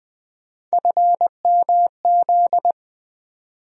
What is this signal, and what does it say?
Morse code, heard as a keyed tone, about 20 wpm, 700 Hz. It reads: FMZ
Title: USA Call Sign Postfixes - 3 Letters 20wpm